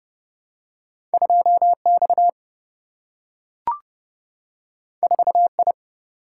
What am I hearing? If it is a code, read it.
Morse code, heard as a keyed tone, about 30 wpm, 700 Hz. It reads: 2 X 4 I
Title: Single-Letter Single-Number 30wpm